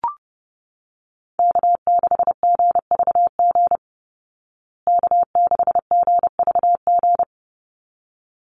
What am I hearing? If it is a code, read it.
Morse code, heard as a keyed tone, about 28 wpm, 700 Hz. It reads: K6GVG K6GVG
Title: Call Signs Repeated 3x - 30wpm